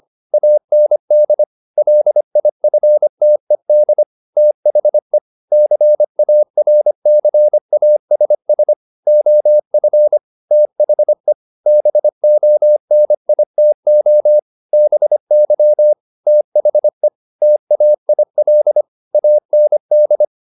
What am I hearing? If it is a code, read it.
Morse code, heard as a keyed tone, about 25 wpm, 600 Hz. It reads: AND LIFTED THE CARCASS OF THE BONITO BY THE TAIL AND
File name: the old man and the sea part 1 25wpm